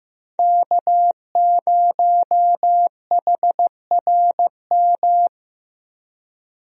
Morse code, heard as a keyed tone, about 15 wpm, 700 Hz. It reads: K0HRM
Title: USA Call Signs - 5 Characters - Repeated 2x - 15wpm